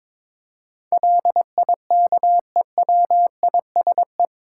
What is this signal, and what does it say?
Morse code, heard as a keyed tone, about 22 wpm, 700 Hz. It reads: LIKEWISE
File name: practice / 22wpm / Eight Letter Words - 22wpm